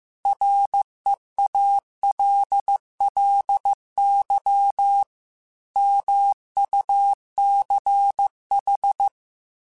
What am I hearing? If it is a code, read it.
Morse code, heard as a keyed tone, about 15 wpm, 800 Hz. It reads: REALLY MUCH
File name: QSOs / QSOs-mp3s / WD0BGP 1@10